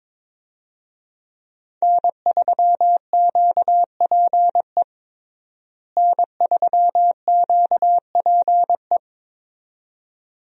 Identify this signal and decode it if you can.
Morse code, heard as a keyed tone, about 21 wpm, 700 Hz. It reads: N3QPE N3QPE
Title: USA Call Signs - 5 Characters - Repeated 2x - 22wpm